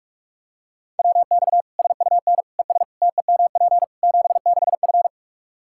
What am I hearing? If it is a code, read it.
Morse code, heard as a keyed tone, about 46 wpm, 700 Hz. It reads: WX SUN ES TEMP 76F
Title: K1USN SST QSO Practice 45wpm